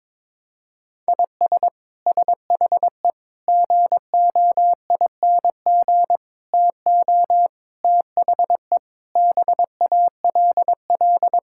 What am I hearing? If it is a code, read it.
Morse code, heard as a keyed tone, about 22 wpm, 700 Hz. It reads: IS SHE GOING TO THE BALL
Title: Sentences from Top 1000 Words - Review Part 6 - 22wpm